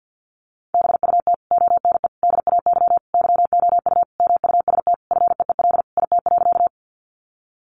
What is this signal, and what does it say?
Morse code, heard as a keyed tone, about 50 wpm, 700 Hz. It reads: BUT ONE DAY YOU MUST FEEL IT.
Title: Sentences from Top 400 Words 50wpm